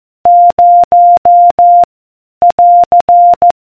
Morse code, heard as a keyed tone, about 14 wpm, 700 Hz. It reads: 0 <AR>